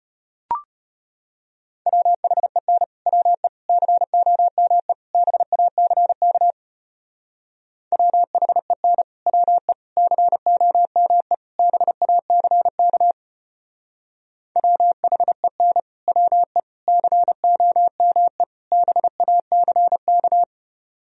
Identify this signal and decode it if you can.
Morse code, heard as a keyed tone, about 30 wpm, 700 Hz. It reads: WHEN WE COME BACK WHEN WE COME BACK WHEN WE COME BACK
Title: Speed-Racing - Sets of 4 Words 25wpm